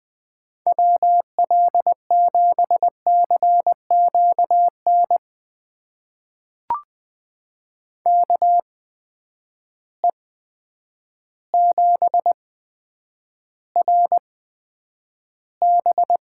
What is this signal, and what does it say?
Morse code, heard as a keyed tone, about 21 wpm, spaced as for 9 wpm, 700 Hz. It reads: WL7CQN KE7RB
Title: USA Call Signs - 6 Characters - 8x Character Spacing - 20wpm